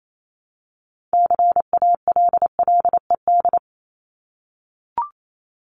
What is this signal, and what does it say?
Morse code, heard as a keyed tone, about 28 wpm, 700 Hz. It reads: CALLED
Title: ED Suffix - Top 100 Words with Base Words 28wpm